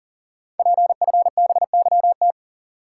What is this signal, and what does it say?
Morse code, heard as a keyed tone, about 38 wpm, 690 Hz. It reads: PFBYT
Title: Code Groups - 5 Letters - 40wpm